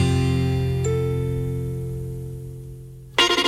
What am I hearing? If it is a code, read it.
Dial tone (North America), 350 + 440 Hz.
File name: wrct-log hi- 15